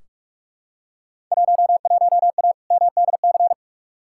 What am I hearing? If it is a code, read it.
Morse code, heard as a keyed tone, about 42 wpm, 700 Hz. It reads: 11A MDC
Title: Field Day Exchanges Encore - 45wpm